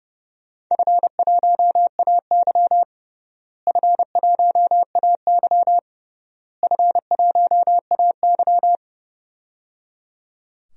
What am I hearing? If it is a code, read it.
Morse code, heard as a keyed tone, about 30 wpm, 700 Hz. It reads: F1AY F1AY F1AY
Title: DXCC-100 Call Signs - 4 Characters - Repeated 3x - 30wpm